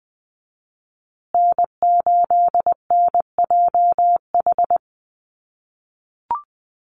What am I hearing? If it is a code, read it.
Morse code, heard as a keyed tone, about 20 wpm, 700 Hz. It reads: N8NJH
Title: USA Call Signs Repeated 3x 20wpm